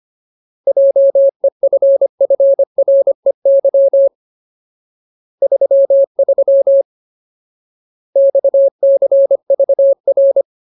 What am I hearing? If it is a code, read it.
Morse code, heard as a keyed tone, about 25 wpm, 550 Hz. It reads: JEFFREY 33 XCVR
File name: Ultimate Word List / 25WPM Ultimate